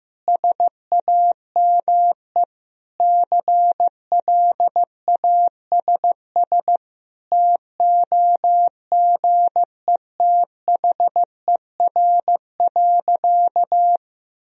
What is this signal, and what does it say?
Morse code, heard as a keyed tone, about 15 wpm, 700 Hz. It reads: SAME CLASS TOGETHER.
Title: Sentences from Top 1000 Words - Review Part 2 - 15wpm